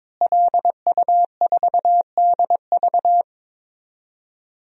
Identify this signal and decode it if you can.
Morse code, heard as a keyed tone, about 22 wpm, 700 Hz. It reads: LU4DV
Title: DXCC-100 Call Signs - 5 Characters - Repeated 2x - 22wpm